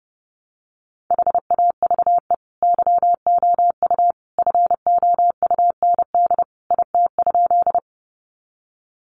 Morse code, heard as a keyed tone, about 30 wpm, 700 Hz. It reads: HAVE YOU FOUND IT?
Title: Sentences from Top 300 Words 30wpm